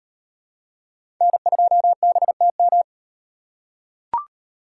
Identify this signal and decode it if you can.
Morse code, heard as a keyed tone, about 38 wpm, 700 Hz. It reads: N2BTM